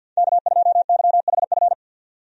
Morse code, heard as a keyed tone, about 50 wpm, 700 Hz. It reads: K2YHF
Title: USA Call Signs - 5 Characters - 50wpm